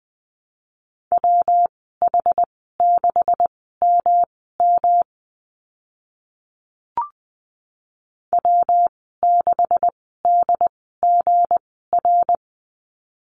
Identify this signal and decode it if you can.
Morse code, heard as a keyed tone, about 20 wpm, spaced as for 14 wpm, 700 Hz. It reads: WH6MM W6DGR
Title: 186 Lesson 30g - Letter b - Call Signs 20wpm